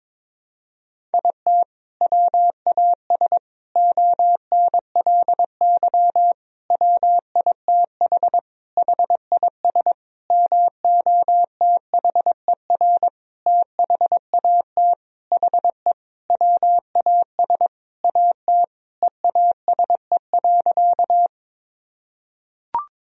Morse code, heard as a keyed tone, about 22 wpm, 700 Hz. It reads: IT WAS ONLY WITH HIS MOTHER THAT HE WAS AT EASE.
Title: Sentences from Top 1000 Words - Review Part 1 - 22wpm